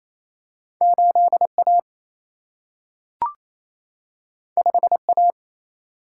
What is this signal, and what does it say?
Morse code, heard as a keyed tone, about 28 wpm, 700 Hz. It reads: 8 A 5 A
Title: Single-Letter Single-Number 28wpm